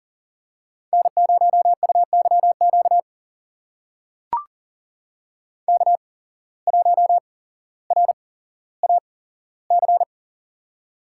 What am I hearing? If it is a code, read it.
Morse code, heard as a keyed tone, about 40 wpm, spaced as for 17 wpm, 700 Hz. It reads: N0UYQ K1RAC
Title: USA Call Signs - 5 Characters - 8x Character Spacing - 40wpm